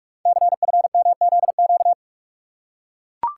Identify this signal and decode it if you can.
Morse code, heard as a keyed tone, about 44 wpm, 690 Hz. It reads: CFMZQ
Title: Code Groups - 5 Letters - 45wpm